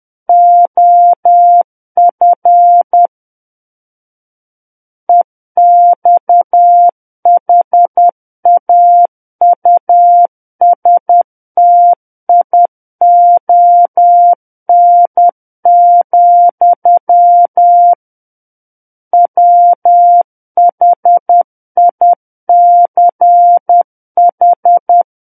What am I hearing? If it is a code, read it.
Morse code, heard as a keyed tone, about 10 wpm, 700 Hz. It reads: OF EXHAUSTION, WHICH
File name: Book One / 01-Chapter 1 - 10 WPM